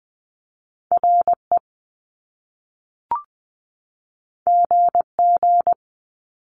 Morse code, heard as a keyed tone, about 20 wpm, 700 Hz. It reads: R E G G
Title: Top Two Letter Combinations in Words 20wpm